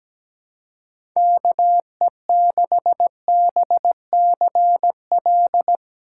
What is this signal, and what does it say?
Morse code, heard as a keyed tone, about 17 wpm, 700 Hz. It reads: KE6BCL
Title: USA Call Signs - 6 Characters - 4x Character Spacing - 17wpm